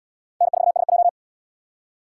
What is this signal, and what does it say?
Morse code, heard as a keyed tone, about 77 wpm, 700 Hz. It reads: N4SJ